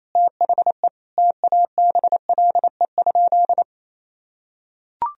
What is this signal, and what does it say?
Morse code, heard as a keyed tone, about 28 wpm, 700 Hz. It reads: THE TABLE?
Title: Sentences from Top 1000 Words - Review Part 2 - 28wpm